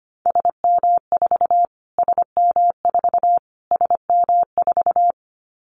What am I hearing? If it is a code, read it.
Morse code, heard as a keyed tone, about 25 wpm, 700 Hz. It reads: SM4 SM4 SM4
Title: Callsign Prefixes for Top 25 DXCC - Repeated 3x - 25wpm